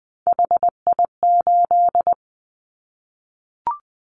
Morse code, heard as a keyed tone, about 20 wpm, 700 Hz. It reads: HI8
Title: Callsign Prefixes for Top 100 DXCC - 20wpm